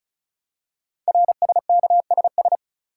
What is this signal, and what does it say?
Morse code, heard as a keyed tone, about 36 wpm, 700 Hz. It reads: RSKSS
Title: Code Groups - 5 Letters - 35wpm